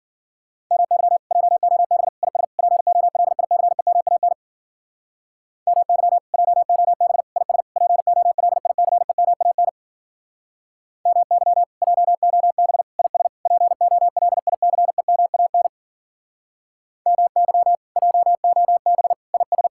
Morse code, heard as a keyed tone, about 50 wpm, 700 Hz. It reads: MY JOB IS POLICEMAN MY JOB IS POLICEMAN MY JOB IS POLICEMAN MY JOB IS
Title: Speed-Racing Common QSO Phrases Encore - 40wpm